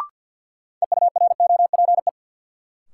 Morse code, heard as a keyed tone, about 52 wpm, 700 Hz. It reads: EUROPE
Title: Sets of 5 Words Encore - New Word Drill - 50wpm